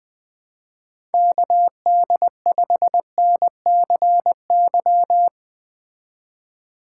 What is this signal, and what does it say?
Morse code, heard as a keyed tone, about 20 wpm, 700 Hz. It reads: KD5NCY
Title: Speed Racing - USA Call Signs 17wpm